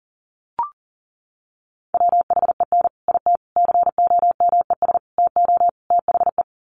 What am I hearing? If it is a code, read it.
Morse code, heard as a keyed tone, about 40 wpm, 700 Hz. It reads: WHEN IT COMES TO THE
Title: Sets of 5 Words 40wpm